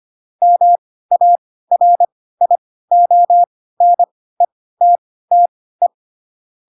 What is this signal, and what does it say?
Morse code, heard as a keyed tone, about 25 wpm, spaced as for 16 wpm, 700 Hz. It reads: MARIONETTE
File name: Pinocchio / Pinocchio ch3 10 25 ik1hge